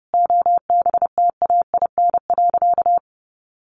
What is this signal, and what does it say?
Morse code, heard as a keyed tone, about 30 wpm, 700 Hz. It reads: OBTAIN.
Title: Sentences from Top 1300 Words 30wpm